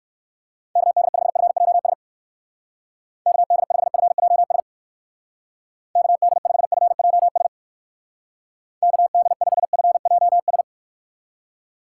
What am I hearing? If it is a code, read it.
Morse code, heard as a keyed tone, about 59 wpm, 700 Hz. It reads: KD5FJS KD5FJS KD5FJS KD5FJS